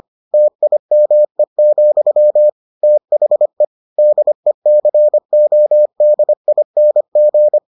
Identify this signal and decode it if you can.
Morse code, heard as a keyed tone, about 25 wpm, 600 Hz. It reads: TIME, THE DECODING